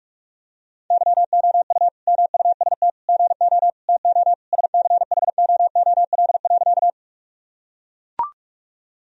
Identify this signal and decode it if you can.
Morse code, heard as a keyed tone, about 42 wpm, 700 Hz. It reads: YOU MUST GO TO SCHOOL.
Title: Sentences from Top 1000 Words - Review Part 1 - 45wpm